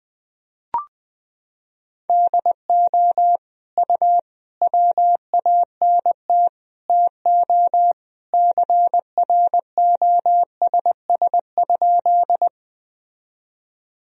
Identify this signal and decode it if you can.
Morse code, heard as a keyed tone, about 21 wpm, 700 Hz. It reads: DO U WANT TO CROSS?
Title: Sentences from Top 300 Words and Abbreviations 20wpm